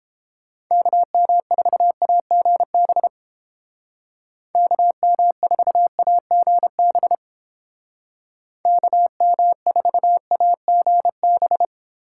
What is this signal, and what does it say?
Morse code, heard as a keyed tone, about 30 wpm, 700 Hz. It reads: KM4AGB KM4AGB KM4AGB